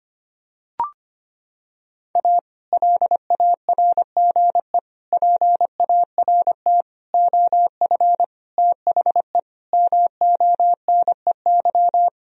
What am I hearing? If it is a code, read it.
Morse code, heard as a keyed tone, about 25 wpm, 700 Hz. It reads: A LARGE PART OF THE MONEY
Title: Sentences from Top 1000 Words - Review Part 1 - 25wpm